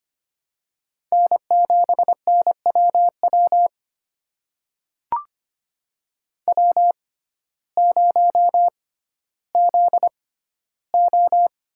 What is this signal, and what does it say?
Morse code, heard as a keyed tone, about 25 wpm, spaced as for 14 wpm, 700 Hz. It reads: N7NWW W0ZO
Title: USA Call Signs - 5 Characters - 6x Character Spacing - 25wpm